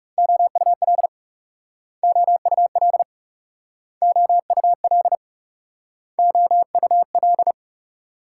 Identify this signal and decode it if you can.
Morse code, heard as a keyed tone, about 38 wpm, 700 Hz. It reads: OUL OUL OUL OUL